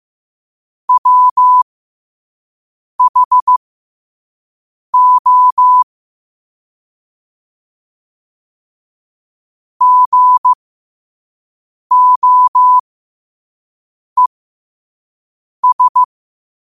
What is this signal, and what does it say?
Morse code, heard as a keyed tone, about 15 wpm, spaced as for 5 wpm, 1000 Hz. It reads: WHO GOES
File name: qotdfiles / 5wpm